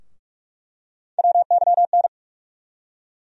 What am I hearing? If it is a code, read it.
Morse code, heard as a keyed tone, about 43 wpm, 700 Hz. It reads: WYN
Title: USA Call Sign Postfixes - 3 Letters 45wpm